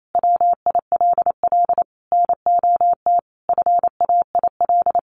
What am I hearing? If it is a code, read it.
Morse code, heard as a keyed tone, about 28 wpm, 700 Hz. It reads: WILL NOT FAIL
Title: Sentences from Top 1100 Words 28wpm